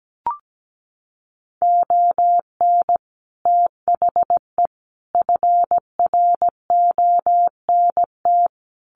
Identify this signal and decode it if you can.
Morse code, heard as a keyed tone, about 17 wpm, 700 Hz. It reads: ON THE FRONT
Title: Sets of 5 Words Encore 17wpm